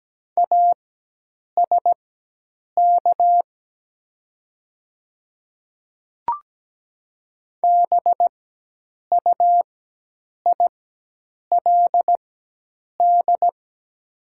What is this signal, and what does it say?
Morse code, heard as a keyed tone, about 17 wpm, spaced as for 8 wpm, 700 Hz. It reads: ASK BUILD
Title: 4x Character Spacing - Top 200 Words - 17wpm